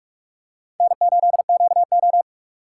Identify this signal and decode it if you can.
Morse code, heard as a keyed tone, about 42 wpm, 700 Hz. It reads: N8QO
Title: USA Call Signs - 6x Character Spacing - 45wpm